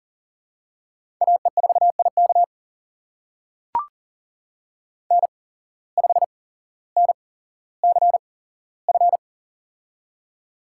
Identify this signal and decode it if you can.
Morse code, heard as a keyed tone, about 42 wpm, spaced as for 17 wpm, 700 Hz. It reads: AE4IK N5NCF
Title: USA Call Signs - 5 Characters - 8x Character Spacing - 40wpm